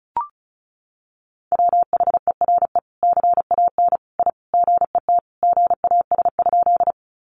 Morse code, heard as a keyed tone, about 34 wpm, 700 Hz. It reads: WHERE CAN I GET GAS?
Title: Sentences from Top 1100 Words 35wpm